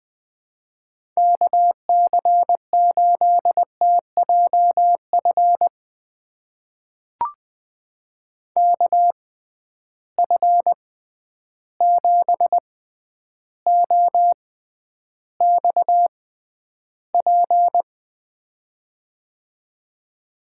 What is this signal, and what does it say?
Morse code, heard as a keyed tone, about 20 wpm, spaced as for 10 wpm, 700 Hz. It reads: KC8TJF KF7OXP
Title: USA Call Signs - 6 Characters - 6x Character Spacing - 20wpm